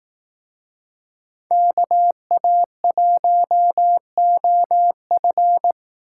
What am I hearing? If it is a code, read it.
Morse code, heard as a keyed tone, about 18 wpm, 700 Hz. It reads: KA1OF
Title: Speed Racing - USA 2x2 Call Signs 15wpm